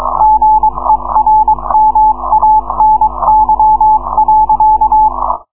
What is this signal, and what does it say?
Morse code, heard as a keyed tone, about 22 wpm, 860 Hz. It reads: GERMANY.